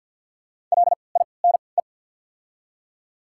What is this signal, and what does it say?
Morse code, heard as a keyed tone, about 54 wpm, spaced as for 29 wpm, 700 Hz. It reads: LINE
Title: 3x Character Spacing - Top 200 Words - 50wpm